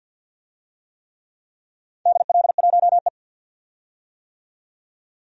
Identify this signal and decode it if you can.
Morse code, heard as a keyed tone, about 52 wpm, 700 Hz. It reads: DL1E
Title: DXCC-100 Call Signs - 4 Characters - 50wpm